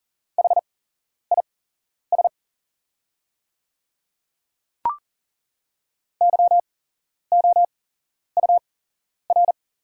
Morse code, heard as a keyed tone, about 40 wpm, spaced as for 11 wpm, 700 Hz. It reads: HIS YOUR
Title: 8x Character Spacing - Top 100 Words - 40wpm